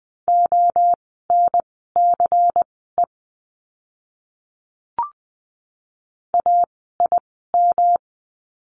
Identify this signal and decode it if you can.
Morse code, heard as a keyed tone, about 20 wpm, spaced as for 15 wpm, 700 Hz. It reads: ONCE AIM
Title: 108 Lesson 18f - Letter m - Words 20wpm